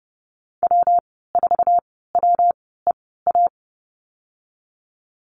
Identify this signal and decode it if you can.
Morse code, heard as a keyed tone, about 30 wpm, spaced as for 17 wpm, 700 Hz. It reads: W4WEA
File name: Lesson 19g - Letter w - Call Signs 30wpm